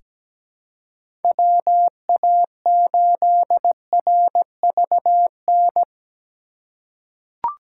Morse code, heard as a keyed tone, about 17 wpm, 700 Hz. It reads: WA8RVN
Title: USA Call Signs - 6 Characters - 4x Character Spacing - 17wpm